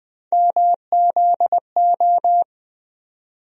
Morse code, heard as a keyed tone, about 20 wpm, 700 Hz. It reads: MZO
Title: USA Call Sign Postfixes - 3 Letters 20wpm